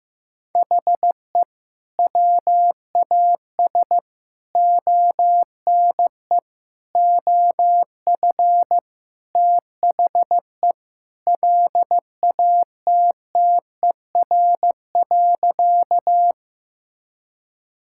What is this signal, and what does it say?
Morse code, heard as a keyed tone, about 16 wpm, 700 Hz. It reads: HE WAS ONE OF THE LATTER.
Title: Sentences from Top 1600 Words 15wpm